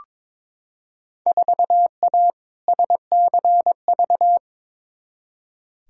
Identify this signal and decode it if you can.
Morse code, heard as a keyed tone, about 22 wpm, 700 Hz. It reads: 4A SCV
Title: Field Day Exchanges - 22wpm